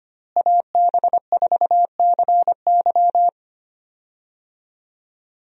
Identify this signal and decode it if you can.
Morse code, heard as a keyed tone, about 25 wpm, 700 Hz. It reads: AB4CY
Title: USA Call Signs - 5 Characters - Repeated 2x - 25wpm